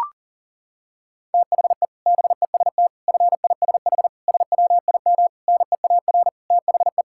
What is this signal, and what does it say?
Morse code, heard as a keyed tone, about 37 wpm, 700 Hz. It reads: THE BEST FISH SWIM NEAR THE
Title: Sentences from Top 1000 Words - Review Part 6 - 40wpm